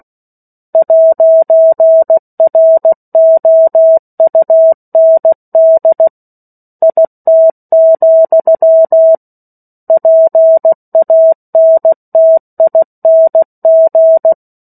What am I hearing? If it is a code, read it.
Morse code, heard as a keyed tone, about 16 wpm, 650 Hz. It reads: 'ROUND IT, PANTING
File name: 03-Chapter-03 16 WPM